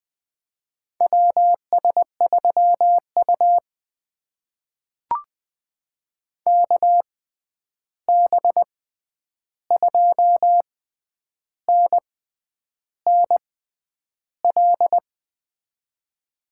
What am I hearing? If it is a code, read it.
Morse code, heard as a keyed tone, about 20 wpm, spaced as for 9 wpm, 700 Hz. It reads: WS3U KB2NNL
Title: USA Call Signs - 6x Character Spacing - 20wpm